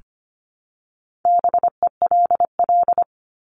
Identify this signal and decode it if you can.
Morse code, heard as a keyed tone, about 25 wpm, 700 Hz. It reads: BELL
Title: Top 1751-2000 Words 25wpm